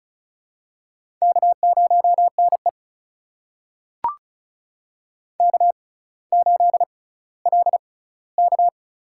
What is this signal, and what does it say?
Morse code, heard as a keyed tone, about 36 wpm, spaced as for 18 wpm, 700 Hz. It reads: K0NE K8LK
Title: USA Call Signs - 4 Characters - 6x Character Spacing - 35wpm